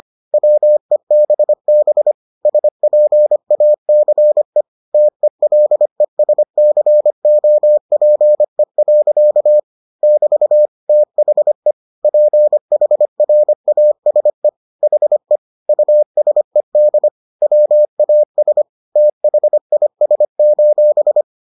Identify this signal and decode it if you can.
Morse code, heard as a keyed tone, about 25 wpm, 600 Hz. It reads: WEBB SPACE TELESCOPE. <BT> THE PHRASE HE USED WAS THIS: